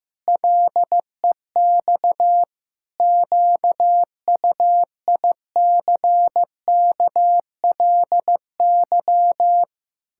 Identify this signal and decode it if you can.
Morse code, heard as a keyed tone, about 14 wpm, 700 Hz. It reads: LEX QUICKLY
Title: Pangram Challenge v2 - 15wpm